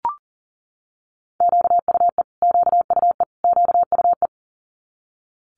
Morse code, heard as a keyed tone, about 40 wpm, 700 Hz. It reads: QUE QUE QUE
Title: Multi-letter Phonemes Repeated 3x 40wpm